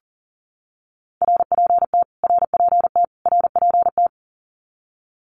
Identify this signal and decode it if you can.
Morse code, heard as a keyed tone, about 40 wpm, 700 Hz. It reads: RPT RPT RPT
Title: Abbreviations Repeated 3x - 40wpm